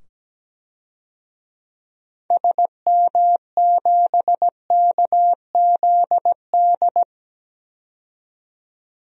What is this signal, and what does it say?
Morse code, heard as a keyed tone, about 17 wpm, 700 Hz. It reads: SM7KZD
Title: DXCC-100 Call Signs - 6 Characters - 17wpm